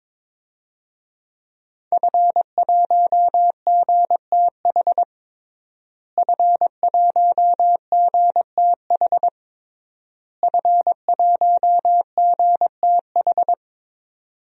Morse code, heard as a keyed tone, about 22 wpm, 700 Hz. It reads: F1GTH F1GTH F1GTH
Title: DXCC-100 Call Signs - 5 Characters - Repeated 3x - 22wpm